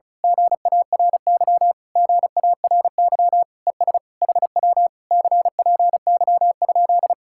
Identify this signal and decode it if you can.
Morse code, heard as a keyed tone, about 32 wpm, 700 Hz. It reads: GARY GARY ES HW CPY?
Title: Speed-Racing Common QSO Phrases - 35wpm